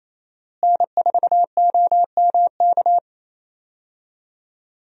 Morse code, heard as a keyed tone, about 28 wpm, 700 Hz. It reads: N4OMK
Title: USA Call Signs - 5 Characters - Repeated 3x - 28wpm